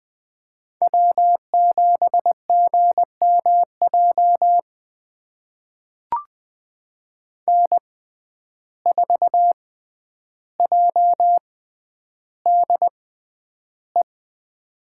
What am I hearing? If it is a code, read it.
Morse code, heard as a keyed tone, about 20 wpm, spaced as for 10 wpm, 700 Hz. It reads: W7GMJ N4JDE
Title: USA Call Signs - 5 Characters - 6x Character Spacing - 20wpm